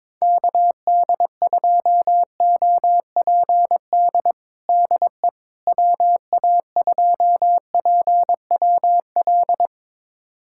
Morse code, heard as a keyed tone, about 22 wpm, 700 Hz. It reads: KD2OPD DE WA2PWL
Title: USA Callsign DE Callsign 22wpm